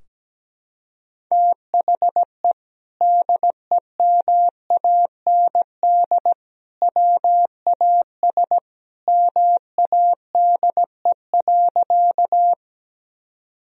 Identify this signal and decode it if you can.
Morse code, heard as a keyed tone, about 17 wpm, 700 Hz. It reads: THE DEMAND WAS MADE.
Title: Sentences from Top 1000 Words - Review Part 4 - 17wpm